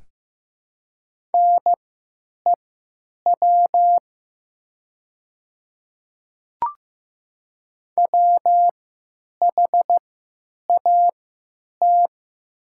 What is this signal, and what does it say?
Morse code, heard as a keyed tone, about 15 wpm, spaced as for 9 wpm, 700 Hz. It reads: NEW WHAT
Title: 3x Character Spacing - Top 300 Words - 15wpm